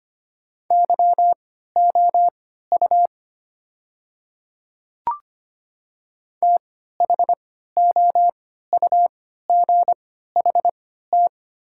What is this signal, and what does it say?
Morse code, heard as a keyed tone, about 25 wpm, spaced as for 14 wpm, 700 Hz. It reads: YOU THOUGHT
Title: 3x Character Spacing - Top 300 Words - 25wpm